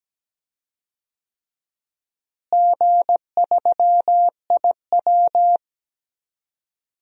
Morse code, heard as a keyed tone, about 17 wpm, 700 Hz. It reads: G3IW
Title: DXCC-100 Call Signs - 4 Characters - Repeated 3x - 17wpm